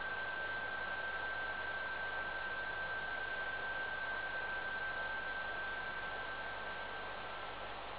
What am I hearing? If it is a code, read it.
PSK31, about 31.25 baud.